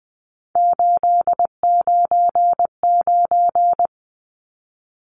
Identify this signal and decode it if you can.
Morse code, heard as a keyed tone, about 20 wpm, 700 Hz. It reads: 899
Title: Numbers - 3-Digit 20wpm